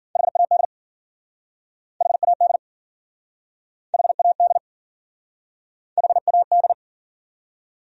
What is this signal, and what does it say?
Morse code, heard as a keyed tone, about 53 wpm, 700 Hz. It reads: HAD HAD HAD HAD